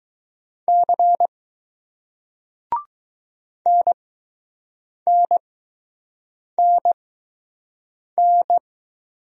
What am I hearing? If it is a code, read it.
Morse code, heard as a keyed tone, about 20 wpm, spaced as for 6 wpm, 700 Hz. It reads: C NNNN